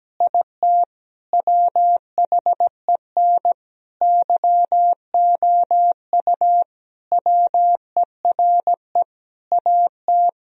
Morse code, heard as a keyed tone, about 17 wpm, 700 Hz. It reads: IT WHEN YOU WERE AT
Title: Sentences from Top 1000 Words - Review Part 1 - 17wpm